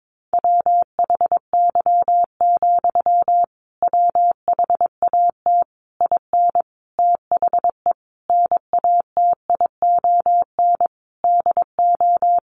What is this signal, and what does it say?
Morse code, heard as a keyed tone, about 22 wpm, 700 Hz. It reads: WHY, WHAT IN THE NATION DO
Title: Sentences from Top 900 Words 22wpm